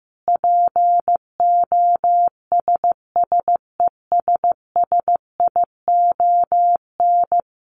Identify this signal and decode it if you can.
Morse code, heard as a keyed tone, about 15 wpm, 700 Hz. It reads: POSSESSION
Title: Sentences from Top 1500 Words 15wpm